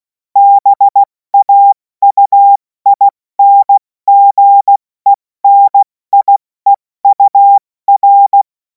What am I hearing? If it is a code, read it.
Morse code, heard as a keyed tone, about 16 wpm, 800 Hz. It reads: BAUINGENIEUR
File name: Morsekurs / Witze / Witz 14. April 2026 70 F80